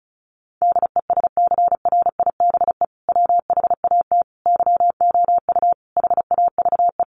Morse code, heard as a keyed tone, about 34 wpm, 700 Hz. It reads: DESCRIBE WHAT YOU HAVE
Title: Sentences from Top 1000 Words 35wpm